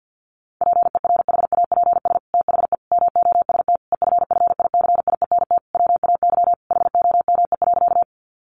Morse code, heard as a keyed tone, about 48 wpm, 700 Hz. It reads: PERHAPS THE MOST EFFICIENT WAY HOME.
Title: Sentences from Top 1300 Words 50wpm